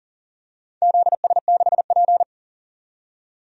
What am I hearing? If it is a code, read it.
Morse code, heard as a keyed tone, about 40 wpm, 700 Hz. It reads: ZS6P
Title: DXCC-100 Call Signs - 4 Characters - 40wpm